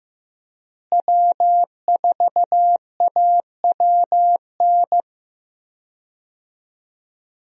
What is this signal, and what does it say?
Morse code, heard as a keyed tone, about 15 wpm, 700 Hz. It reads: W4AWN
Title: USA Call Signs - 5 Characters - Repeated 3x - 15wpm